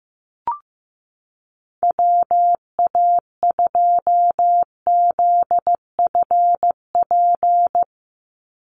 Morse code, heard as a keyed tone, about 14 wpm, 700 Hz. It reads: WA2ZFP
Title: USA Call Signs 15wpm